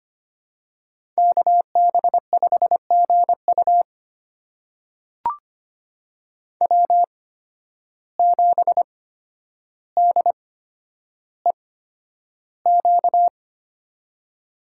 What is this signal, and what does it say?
Morse code, heard as a keyed tone, about 25 wpm, spaced as for 11 wpm, 700 Hz. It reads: KB5GU W7DEQ
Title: USA Call Signs - 5 Characters - 8x Character Spacing - 25wpm